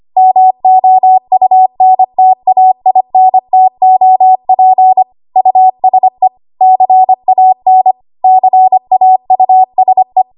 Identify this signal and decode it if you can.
Morse code, heard as a keyed tone, about 25 wpm, 750 Hz. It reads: MOUNTAINTOP USE CAN CAUSE
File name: Morse / 25WPM